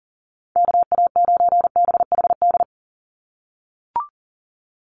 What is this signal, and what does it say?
Morse code, heard as a keyed tone, about 39 wpm, 700 Hz. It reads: KA9BHD
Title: Call Signs - 40wpm